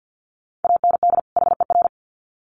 Morse code, heard as a keyed tone, about 55 wpm, 700 Hz. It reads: AND HER
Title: Sets of 2 Words 50wpm